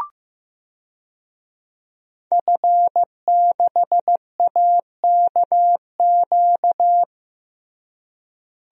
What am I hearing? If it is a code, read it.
Morse code, heard as a keyed tone, about 15 wpm, 700 Hz. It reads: F6AKQ
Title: DXCC-100 Call Signs - 5 Characters - Repeated 3x - 15wpm